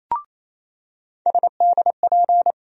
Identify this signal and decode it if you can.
Morse code, heard as a keyed tone, about 28 wpm, 700 Hz. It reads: SDP
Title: USA Call Sign Postfixes - 3 Letters 28wpm